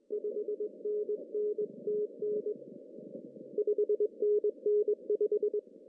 Morse code, heard as a keyed tone, about 21 wpm, 410 Hz. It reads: 5NN TN 5NN 5
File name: CW / cqww-cw-k5tr